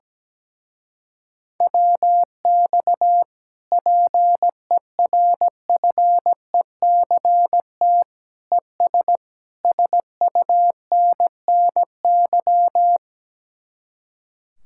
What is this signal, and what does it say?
Morse code, heard as a keyed tone, about 17 wpm, 700 Hz. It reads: WX PERFECT ES SUNNY
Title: Speed-Racing Common QSO Phrases Encore - 17wpm